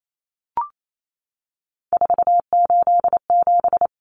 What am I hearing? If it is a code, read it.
Morse code, heard as a keyed tone, about 28 wpm, 700 Hz. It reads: 487
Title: Common Cut Numbers - 3-Digit 28wpm